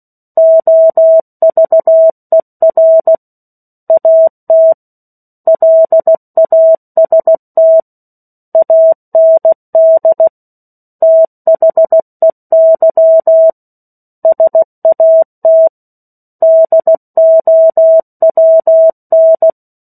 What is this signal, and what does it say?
Morse code, heard as a keyed tone, about 16 wpm, 650 Hz. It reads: OVER AT LAST AND THEY SAT DOWN
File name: 03-Chapter-03 16 WPM